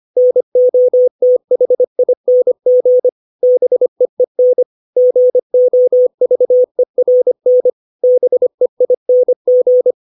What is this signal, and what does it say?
Morse code, heard as a keyed tone, about 25 wpm, 500 Hz. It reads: NOTHING BEEN GOVERN BEING
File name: 632 words / 632 Words 25WPM